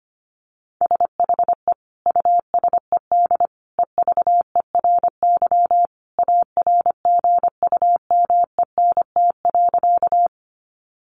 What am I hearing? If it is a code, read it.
Morse code, heard as a keyed tone, about 25 wpm, 700 Hz. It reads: SHE USED EVERY ARGUMENT.
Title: Sentences from Top 1400 Words 25wpm